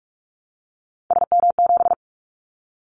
Morse code, heard as a keyed tone, about 46 wpm, 700 Hz. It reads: SM7
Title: Callsign Prefixes for Top 25 DXCC - 45wpm